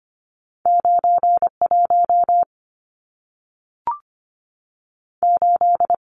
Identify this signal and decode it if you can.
Morse code, heard as a keyed tone, about 25 wpm, 700 Hz. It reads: 91 8
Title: Numbers - 2-Digit 25wpm